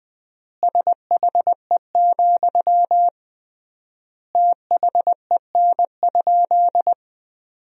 Morse code, heard as a keyed tone, about 20 wpm, 700 Hz. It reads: SHE, THEN?
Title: 3x Word-Spacing - Sentences from Top 200 Words - 20wpm